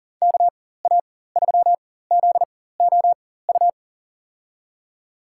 Morse code, heard as a keyed tone, about 40 wpm, spaced as for 19 wpm, 700 Hz. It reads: KA3ZOU
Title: USA Call Signs - 6 Characters - 4x Character Spacing - 40wpm